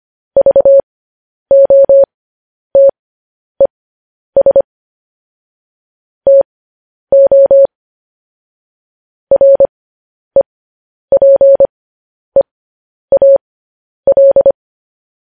Morse code, heard as a keyed tone, about 25 wpm, spaced as for 10 wpm, 550 Hz. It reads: VOTES TO REPEAL